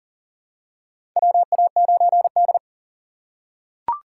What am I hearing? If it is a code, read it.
Morse code, heard as a keyed tone, about 40 wpm, 700 Hz. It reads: WA9D
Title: USA Call Signs - 4 Characters - 6x Character Spacing - 40wpm